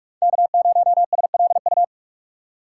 Morse code, heard as a keyed tone, about 43 wpm, 700 Hz. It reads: K0SLU